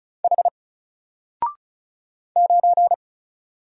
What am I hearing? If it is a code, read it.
Morse code, heard as a keyed tone, about 33 wpm, 700 Hz. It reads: H 9
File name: Single-Letter-Number-35wpm